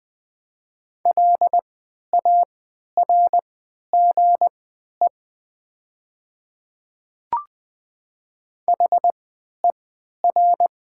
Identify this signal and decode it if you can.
Morse code, heard as a keyed tone, about 20 wpm, spaced as for 11 wpm, 700 Hz. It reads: LARGE HER
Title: 3x Character Spacing - Top 200 Words - 20wpm